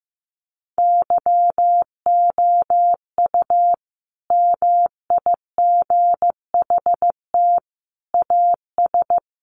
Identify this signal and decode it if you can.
Morse code, heard as a keyed tone, about 15 wpm, 700 Hz. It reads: YOU MIGHT AS
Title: Sentences from Top 1400 Words 15wpm